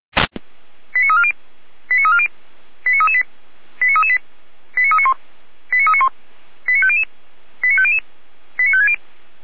Five-tone selective call (ZVEI), heard as quick runs of five tones, tones of about 80 ms. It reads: 89340 89340 89338 89338 89391 89391 89500 89500 89560
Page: /455/SDR/ZVEI/89325/